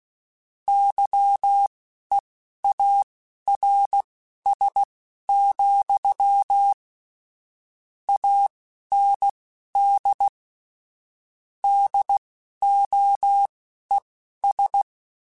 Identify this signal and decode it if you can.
Morse code, heard as a keyed tone, about 16 wpm, spaced as for 11 wpm, 800 Hz. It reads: YEARS, AND DOES
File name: Technical Subjects / Technical Subjects-mp3s / AND Gate @10